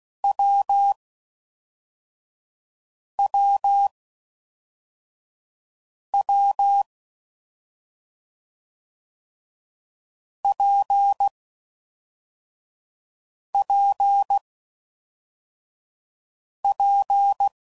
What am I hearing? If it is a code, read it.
Morse code, heard as a keyed tone, about 16 wpm, 800 Hz. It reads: W W W P P P
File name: Combinations / Difficult Combos / WPJ WPJ1